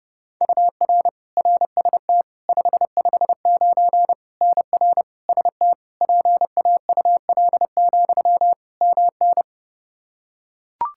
Morse code, heard as a keyed tone, about 30 wpm, 700 Hz. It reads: UR RST 559 NR ST PAUL, MN
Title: Practice QSO - Exchange 1 - Beginning - 30wpm